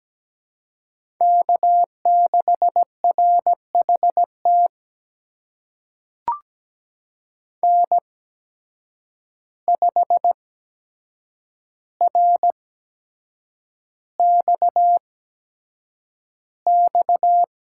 Morse code, heard as a keyed tone, about 17 wpm, spaced as for 7 wpm, 700 Hz. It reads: K6RHT N5RXX
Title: USA Call Signs - 5 Characters - 8x Character Spacing - 17wpm